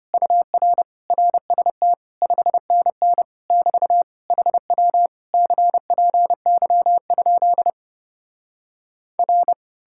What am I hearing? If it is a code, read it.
Morse code, heard as a keyed tone, about 33 wpm, 700 Hz. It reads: UR RST 5NN <BT> HW CPY? R